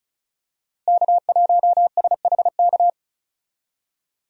Morse code, heard as a keyed tone, about 34 wpm, 700 Hz. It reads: K1SHK
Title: USA Call Signs - 5 Characters - Repeated 2x - 35wpm